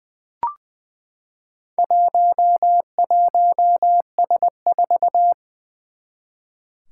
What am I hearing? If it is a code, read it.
Morse code, heard as a keyed tone, about 20 wpm, 700 Hz. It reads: 11S4
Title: Four Challenging Letter-Number Combinations 20wpm